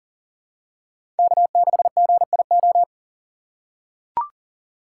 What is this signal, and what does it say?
Morse code, heard as a keyed tone, about 40 wpm, 700 Hz. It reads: K6GIO
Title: USA Call Signs - 6x Character Spacing - 40wpm